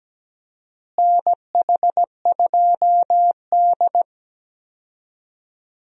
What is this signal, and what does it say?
Morse code, heard as a keyed tone, about 17 wpm, 700 Hz. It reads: NH2D
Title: USA Call Signs - 4 Characters - Repeated 3x - 17wpm